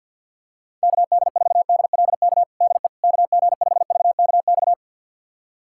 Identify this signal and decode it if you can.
Morse code, heard as a keyed tone, about 51 wpm, 700 Hz. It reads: KD4DLK DE KG5VKX
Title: USA Callsign DE Callsign - 6 Characters - 50wpm